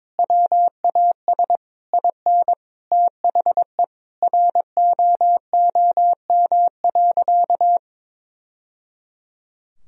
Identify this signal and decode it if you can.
Morse code, heard as a keyed tone, about 22 wpm, 700 Hz. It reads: WAS IN THE ROOM.
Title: Sentences from Top 1000 Words - Review Part 4 - 22wpm